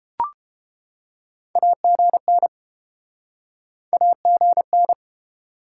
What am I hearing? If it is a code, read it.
Morse code, heard as a keyed tone, about 30 wpm, 700 Hz. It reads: AGN AGN